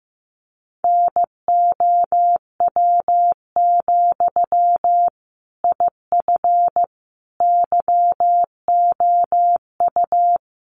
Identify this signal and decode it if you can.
Morse code, heard as a keyed tone, about 15 wpm, 700 Hz. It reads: NOW, IF YOU
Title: Sentences from Top 900 Words 15wpm